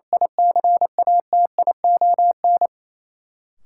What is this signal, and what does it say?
Morse code, heard as a keyed tone, about 28 wpm, 700 Hz. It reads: ICATION
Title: Sentences from Top 1400 Words - New Word Drill Alt - 28wpm